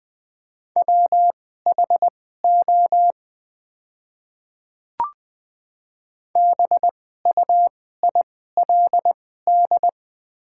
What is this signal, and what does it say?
Morse code, heard as a keyed tone, about 21 wpm, spaced as for 15 wpm, 700 Hz. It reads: WHO BUILD
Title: 2x Character Spacing - Top 200 Words - 20wpm